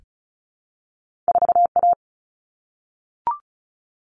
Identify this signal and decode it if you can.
Morse code, heard as a keyed tone, about 36 wpm, 700 Hz. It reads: 4A
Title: Common Cut Numbers - 2-Digit 35wpm